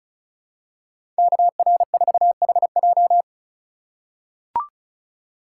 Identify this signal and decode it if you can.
Morse code, heard as a keyed tone, about 36 wpm, 700 Hz. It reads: KR4HJ
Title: USA Call Signs - 5 Characters - 3x Character Spacing - 35wpm